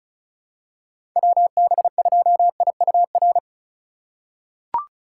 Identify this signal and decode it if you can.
Morse code, heard as a keyed tone, about 34 wpm, 700 Hz. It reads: WB2IUR
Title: USA Call Signs - 6 Characters - 2x Character Spacing - 35wpm